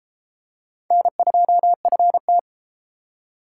Morse code, heard as a keyed tone, about 31 wpm, 700 Hz. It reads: N2FT